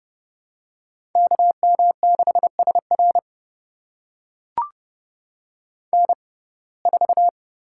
Morse code, heard as a keyed tone, about 30 wpm, spaced as for 22 wpm, 700 Hz. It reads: KM6SR N4
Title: USA Call Signs - 5 Characters - 6x Character Spacing - 30wpm